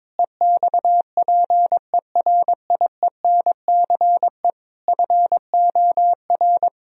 Morse code, heard as a keyed tone, about 22 wpm, 700 Hz. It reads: EXPERIENCE FOR
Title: Sentences from Top 1000 Words - Review Part 4 - 22wpm